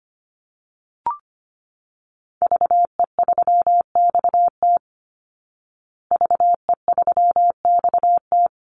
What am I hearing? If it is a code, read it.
Morse code, heard as a keyed tone, about 25 wpm, 700 Hz. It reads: VE3XT VE3XT
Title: Call Signs Repeated 2x - 25wpm